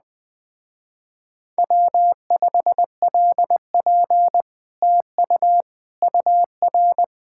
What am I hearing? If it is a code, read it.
Morse code, heard as a keyed tone, about 20 wpm, 700 Hz. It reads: W5LP TU UR
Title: Speed-Racing Common QSO Phrases - 17wpm